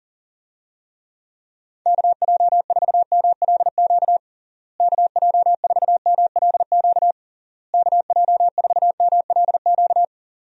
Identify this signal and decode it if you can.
Morse code, heard as a keyed tone, about 40 wpm, 700 Hz. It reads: KJ4MLQ KJ4MLQ KJ4MLQ
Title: USA Call Signs - 6 Characters - Repeated 3x - 40wpm